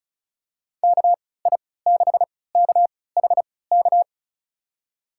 Morse code, heard as a keyed tone, about 36 wpm, spaced as for 20 wpm, 700 Hz. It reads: KI6KHK
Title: USA Call Signs - 3x Character Spacing - 35wpm